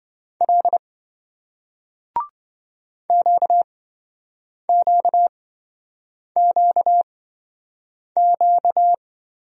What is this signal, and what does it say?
Morse code, heard as a keyed tone, about 27 wpm, spaced as for 7 wpm, 700 Hz. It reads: L QQQQ